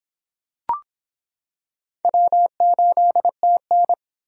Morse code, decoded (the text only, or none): W8TN